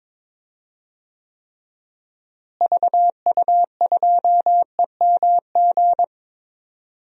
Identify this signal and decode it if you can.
Morse code, heard as a keyed tone, about 21 wpm, 700 Hz. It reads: VU2EMG
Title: DXCC-100 Call Signs - 6 Characters - Repeated 2x - 22wpm